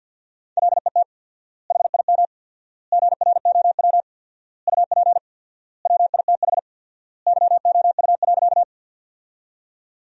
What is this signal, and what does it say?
Morse code, heard as a keyed tone, about 50 wpm, 700 Hz. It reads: LET HIM GROW UP WITH YOU.
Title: 4x Word-Spacing - Sentences from Top 300 Words - 50wpm